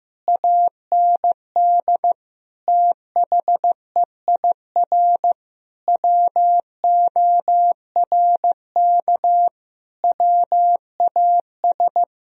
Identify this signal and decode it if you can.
Morse code, heard as a keyed tone, about 15 wpm, 700 Hz. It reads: AND THEIR WORK WAS
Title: Sentences from Top 1000 Words - Review Part 1 - 15wpm